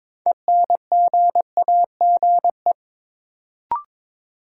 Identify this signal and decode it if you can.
Morse code, heard as a keyed tone, about 22 wpm, 700 Hz. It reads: ENGAGE
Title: Sentences from Top 1500 Words - New Word Drill Alt - 22wpm